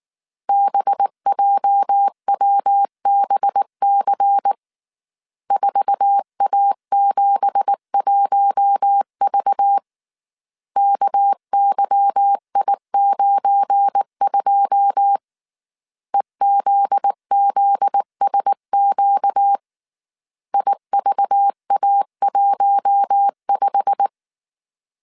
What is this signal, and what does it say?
Morse code, heard as a keyed tone, about 19 wpm, 800 Hz. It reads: BJWBC 4A71V KYI92 EZZSQ IVA15
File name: CW / DET / 1 9 1500C S A 18wpm 2459